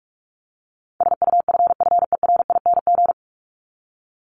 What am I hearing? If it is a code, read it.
Morse code, heard as a keyed tone, about 44 wpm, 700 Hz. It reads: SUFFERING
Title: ING Suffix - Top 100 Words with Base Words 45wpm